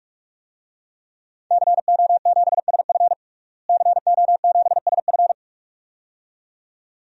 Morse code, heard as a keyed tone, about 43 wpm, 700 Hz. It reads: CO7SF CO7SF
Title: DXCC-100 Call Signs - 5 Characters - Repeated 2x - 45wpm